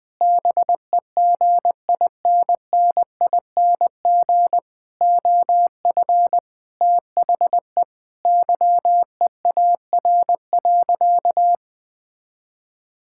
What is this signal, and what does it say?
Morse code, heard as a keyed tone, about 21 wpm, 700 Hz. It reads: BEGINNING OF THE YEAR.
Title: Sentences with 6-Word Phrases - 20wpm